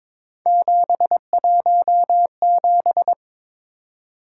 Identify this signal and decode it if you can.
Morse code, heard as a keyed tone, about 26 wpm, 700 Hz. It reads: 717
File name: Speed-Racing - Three Letter-Number Combinations - 22wpm